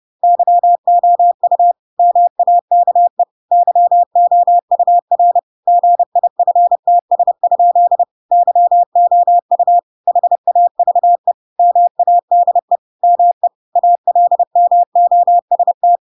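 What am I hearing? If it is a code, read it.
Morse code, heard as a keyed tone, about 30 wpm, 700 Hz. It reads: YOU MAKE YOUR GIFTS? YOU HAVE MADE ME ALMOST
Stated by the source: Chapter 2 of The Adventures of Pinocchio read as a book in Morse code (CW) at 30 wpm, by ik1hge.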